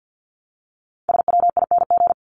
Morse code, heard as a keyed tone, about 48 wpm, 700 Hz. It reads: SWING
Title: Top 1751-2000 Words 50wpm